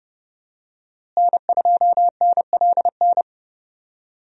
Morse code, heard as a keyed tone, about 30 wpm, 700 Hz. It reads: N2NLN
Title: USA Call Signs - 5 Characters - Repeated 2x - 30wpm